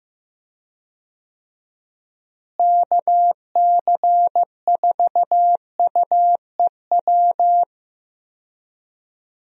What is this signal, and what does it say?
Morse code, heard as a keyed tone, about 15 wpm, 700 Hz. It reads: KC4UEW
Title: USA Call Signs - 6 Characters - Repeated 2x - 15wpm